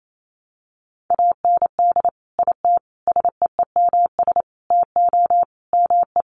Morse code, heard as a keyed tone, about 28 wpm, 700 Hz. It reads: AND IT SEEMS TO ME
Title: Sets of 5 Words Encore2 28wpm